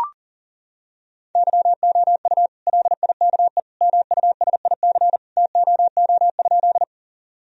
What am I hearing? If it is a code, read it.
Morse code, heard as a keyed tone, about 40 wpm, 700 Hz. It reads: YOU LIKE MUSIC TOO?
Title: Sentences from Top 1000 Words - Review Part 1 - 40wpm